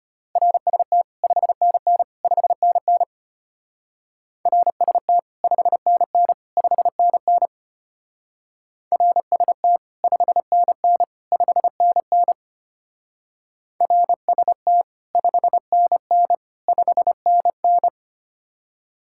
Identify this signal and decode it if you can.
Morse code, heard as a keyed tone, about 30 wpm, 700 Hz. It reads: RST 5NN 5NN RST 5NN 5NN RST 5NN 5NN RST 5NN 5NN
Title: Speed-Racing Common QSO Phrases - 25wpm